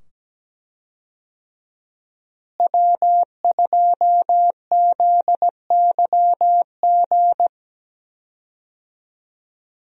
Morse code, heard as a keyed tone, about 17 wpm, 700 Hz. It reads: W2ZYG
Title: USA Call Signs - 5 Characters - Repeated 2x - 17wpm